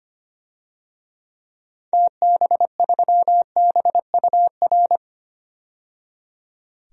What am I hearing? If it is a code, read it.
Morse code, heard as a keyed tone, about 25 wpm, 700 Hz. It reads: TB3BUR
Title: DXCC-100 Call Signs - 6 Characters - 25wpm